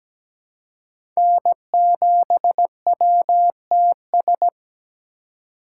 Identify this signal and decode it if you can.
Morse code, heard as a keyed tone, about 17 wpm, 700 Hz. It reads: N7WTS
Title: USA Call Signs - 5 Characters - 4x Character Spacing - 17wpm